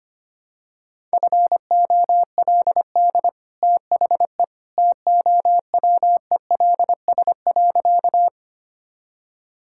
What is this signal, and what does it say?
Morse code, heard as a keyed tone, about 25 wpm, 700 Hz. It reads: FOLD THE TOWELS.
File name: Easy Sentences with Hints - 25wpm